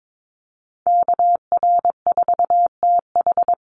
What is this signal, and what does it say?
Morse code, heard as a keyed tone, about 22 wpm, 690 Hz. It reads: KR4TH
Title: Call Signs Repeated 3x - 22wpm